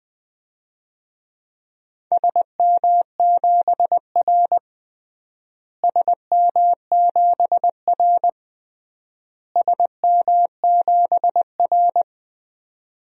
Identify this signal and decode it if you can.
Morse code, heard as a keyed tone, about 20 wpm, 700 Hz. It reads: SM7R SM7R SM7R
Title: DXCC-100 Call Signs - 4 Characters - Repeated 3x - 20wpm